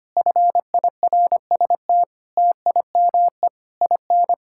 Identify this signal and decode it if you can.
Morse code, heard as a keyed tone, about 25 wpm, 700 Hz. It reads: FIRST TIME IN
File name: practice / 25wpm / Sentences with 4-Word Phrases - 25wpm